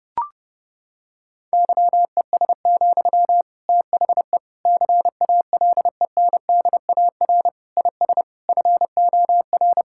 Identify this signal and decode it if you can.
Morse code, heard as a keyed tone, about 30 wpm, 700 Hz. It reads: YES, THE CALENDAR IS FOR
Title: Sentences from Top 1600 Words 30wpm